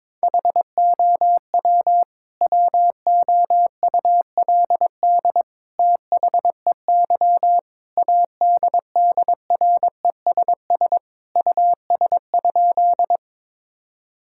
Morse code, heard as a keyed tone, about 22 wpm, 700 Hz. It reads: HOW WOULD THEY ADDRESS US?
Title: Sentences from Top 1000 Words - Review Part 6 - 22wpm